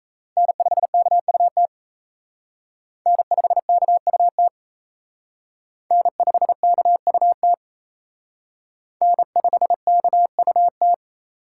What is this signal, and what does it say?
Morse code, heard as a keyed tone, about 34 wpm, 700 Hz. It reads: N5KUT N5KUT N5KUT N5KUT
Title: Speed Racing - USA 1x3 Call Signs 28wpm